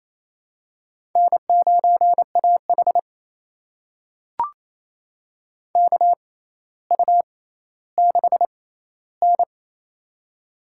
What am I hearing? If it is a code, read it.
Morse code, heard as a keyed tone, about 28 wpm, spaced as for 14 wpm, 700 Hz. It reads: N9AH KU6N
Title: USA Call Signs - 4 Characters - 6x Character Spacing - 28wpm